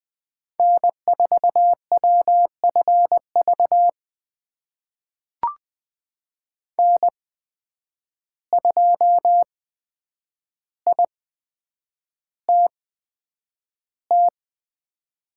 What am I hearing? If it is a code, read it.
Morse code, heard as a keyed tone, about 20 wpm, spaced as for 9 wpm, 700 Hz. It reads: N4WFV N2ITT
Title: USA Call Signs - 5 Characters - 8x Character Spacing - 20wpm